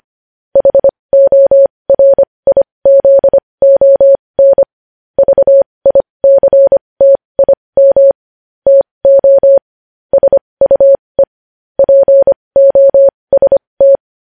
Morse code, heard as a keyed tone, about 25 wpm, spaced as for 20 wpm, 550 Hz. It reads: HORIZON VICTIM TO SUE POST
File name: podcast / News headlines-1x-25x20